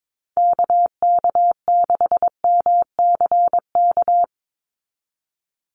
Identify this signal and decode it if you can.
Morse code, heard as a keyed tone, about 22 wpm, 700 Hz. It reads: KK6MCK
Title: USA Call Signs 22wpm